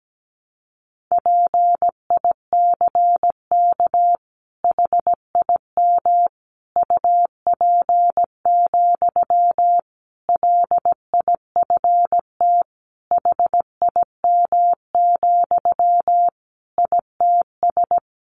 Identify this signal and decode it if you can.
Morse code, heard as a keyed tone, about 17 wpm, 700 Hz. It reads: PICK HIM UP, LIFT HIM, ITS
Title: Sentences from Top 1000 Words 17wpm